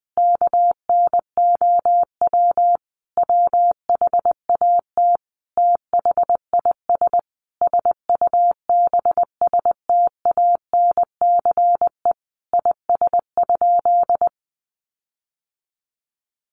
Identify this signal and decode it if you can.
Morse code, heard as a keyed tone, about 20 wpm, 700 Hz. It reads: KNOW WHAT THIS SUBSTANCE IS?
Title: Sentences from Top 1500 Words 20wpm